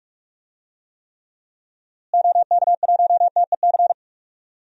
Morse code, heard as a keyed tone, about 41 wpm, 700 Hz. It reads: OK1TEC
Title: DXCC-100 Call Signs - 6 Characters - Repeated 3x - 45wpm